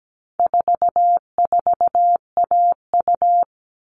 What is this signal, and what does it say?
Morse code, heard as a keyed tone, about 17 wpm, 700 Hz. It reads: 44AU
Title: All Cut Numbers - 4-Digit 17wpm